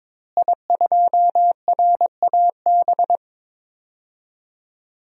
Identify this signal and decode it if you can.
Morse code, heard as a keyed tone, about 22 wpm, 700 Hz. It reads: I2RAB
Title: DXCC-100 Call Signs - 5 Characters - Repeated 3x - 22wpm